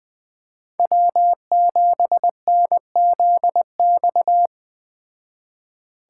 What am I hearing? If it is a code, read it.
Morse code, heard as a keyed tone, about 20 wpm, 700 Hz. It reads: W7NZX